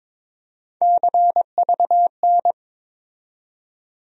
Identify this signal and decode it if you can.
Morse code, heard as a keyed tone, about 22 wpm, 700 Hz. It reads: CVN
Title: USA Call Sign Postfixes - 3 Letters 22wpm